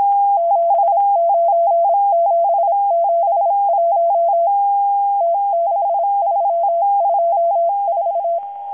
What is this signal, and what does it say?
Morse code, heard as a keyed tone, about 26 wpm, 680 Hz. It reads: 79771 T6324
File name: HM02- -0437utc-7351khz